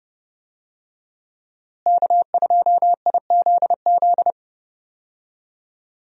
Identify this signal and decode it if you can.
Morse code, heard as a keyed tone, about 30 wpm, 700 Hz. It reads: K2IZZ
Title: USA Call Signs - 5 Characters - Repeated 2x - 30wpm